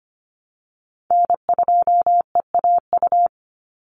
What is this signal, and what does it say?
Morse code, heard as a keyed tone, about 25 wpm, 700 Hz. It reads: N2EAU
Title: USA Call Signs 25wpm